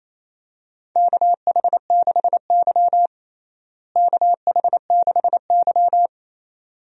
Morse code, heard as a keyed tone, about 28 wpm, 700 Hz. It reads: KH6Y KH6Y
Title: DXCC-100 Call Signs - 4 Characters - Repeated 2x - 28wpm